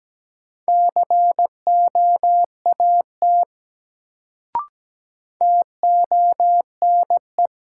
Morse code, heard as a keyed tone, about 17 wpm, 700 Hz. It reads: COAT TONE
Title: Sentences from Top 1600 Words - New Word Drill Alt - 17wpm